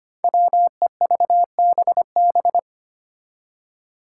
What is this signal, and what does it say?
Morse code, heard as a keyed tone, about 25 wpm, 700 Hz. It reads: WEVBB
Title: Five Challenging Letter-Number Combinations 25wpm